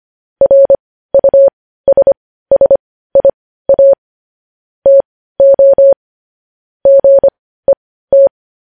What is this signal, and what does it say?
Morse code, heard as a keyed tone, about 25 wpm, spaced as for 15 wpm, 550 Hz. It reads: RUSSIA TO GET
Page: news-headlines-in-morse-code-at-15-wpm